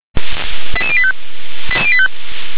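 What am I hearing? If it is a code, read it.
Five-tone selective call (ZVEI), heard as quick runs of five tones, tones of about 70 ms. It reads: 88085
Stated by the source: Received on SDR as ZVEI five-tone selective call, address 88152.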